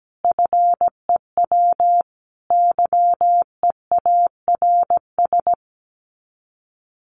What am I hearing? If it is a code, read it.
Morse code, heard as a keyed tone, about 17 wpm, 700 Hz. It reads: FEW YEARS
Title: Sets of 5 Words Encore 17wpm